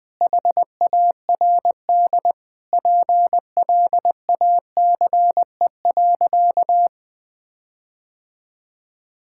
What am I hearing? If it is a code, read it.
Morse code, heard as a keyed tone, about 21 wpm, 700 Hz. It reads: HARD PLACE.
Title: Sentences with 8-Word Phrases - 20wpm